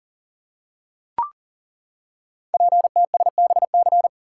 Morse code, heard as a keyed tone, about 41 wpm, 690 Hz. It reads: PTSBC
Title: Code Groups - 5 Letters - 40wpm